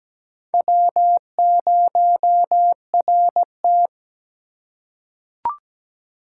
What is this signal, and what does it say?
Morse code, heard as a keyed tone, about 17 wpm, 700 Hz. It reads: W0RT
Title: USA Call Signs - 4 Characters - 3x Character Spacing - 17wpm